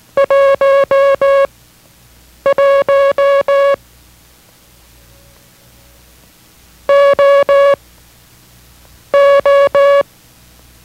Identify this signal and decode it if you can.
Morse code, heard as a keyed tone, about 18 wpm, spaced as for 6 wpm, 560 Hz. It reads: 11 OO